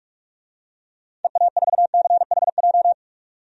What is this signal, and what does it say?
Morse code, heard as a keyed tone, about 46 wpm, 700 Hz. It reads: EA4CHJ
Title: DXCC-100 Call Signs - 6 Characters - 45wpm